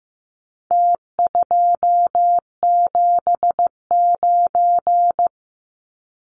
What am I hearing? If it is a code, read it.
Morse code, heard as a keyed tone, about 15 wpm, 700 Hz. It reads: T279
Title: Numbers - Contest Style - 4-Digit 15wpm